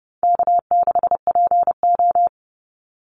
Morse code, heard as a keyed tone, about 30 wpm, 700 Hz. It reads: K6PO
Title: Call Signs Repeated 2x - 30wpm